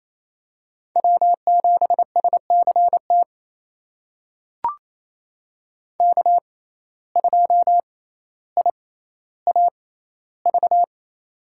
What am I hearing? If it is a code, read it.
Morse code, heard as a keyed tone, about 28 wpm, spaced as for 14 wpm, 700 Hz. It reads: W7SCT K2IAV
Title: USA Call Signs - 5 Characters - 6x Character Spacing - 28wpm